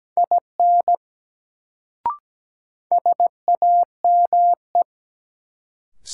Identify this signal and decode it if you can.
Morse code, heard as a keyed tone, about 17 wpm, 700 Hz. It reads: IN SAME
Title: Sets of 2 Words - New Word Drill - 17wpm